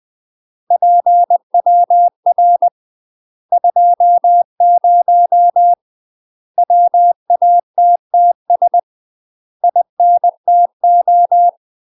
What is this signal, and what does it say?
Morse code, heard as a keyed tone, about 20 wpm, 700 Hz. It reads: PWR 20 WATTS INTO
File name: MASTER QSOs TRAINING/20WPM